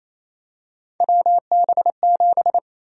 Morse code, heard as a keyed tone, about 28 wpm, 700 Hz. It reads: WB7
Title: USA Call Sign Prefixes - 28wpm